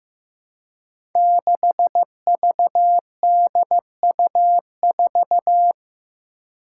Morse code, heard as a keyed tone, about 15 wpm, 700 Hz. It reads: 6VDU4
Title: Five Challenging Letter-Number Combinations 15wpm